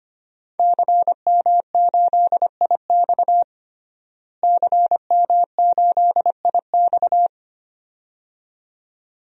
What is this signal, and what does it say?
Morse code, heard as a keyed tone, about 25 wpm, 700 Hz. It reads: CM8IX CM8IX
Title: DXCC-100 Call Signs - 5 Characters - Repeated 2x - 25wpm